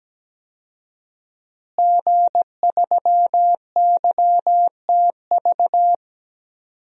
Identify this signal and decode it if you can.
Morse code, heard as a keyed tone, about 17 wpm, 700 Hz. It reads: G3YTV
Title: DXCC-100 Call Signs - 5 Characters - Repeated 3x - 17wpm